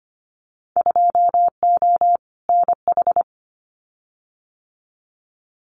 Morse code, heard as a keyed tone, about 25 wpm, 700 Hz. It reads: 2O NH
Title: Winter Field Day Exchanges 25wpm